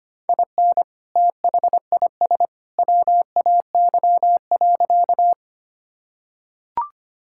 Morse code, heard as a keyed tone, about 25 wpm, 700 Hz. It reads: IN THIS WAY.
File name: practice / 25wpm / Sentences with 2-Word Phrases - 25wpm